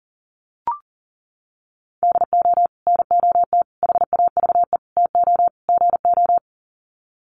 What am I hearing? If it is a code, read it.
Morse code, heard as a keyed tone, about 40 wpm, 700 Hz. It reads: DO NOT HAVE TO GO
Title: Sets of 5 Words 40wpm